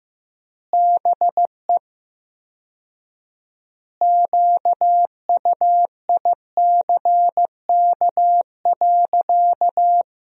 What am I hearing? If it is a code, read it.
Morse code, heard as a keyed tone, about 16 wpm, 700 Hz. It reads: BE QUICK.
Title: 4x Word-Spacing - Sentences from Top 500 Words - 15wpm